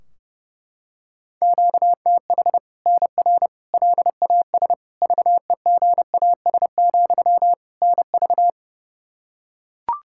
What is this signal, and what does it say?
Morse code, heard as a keyed tone, about 30 wpm, 700 Hz. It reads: QTH NR LAS VEGAS, NV
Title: Practice QSO - Exchange 1 - Beginning - 30wpm